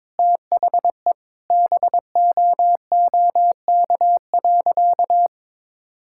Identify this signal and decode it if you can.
Morse code, heard as a keyed tone, about 21 wpm, 700 Hz. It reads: THE BOOK.
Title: Sentences from Top 1000 Words - Review Part 6 - 22wpm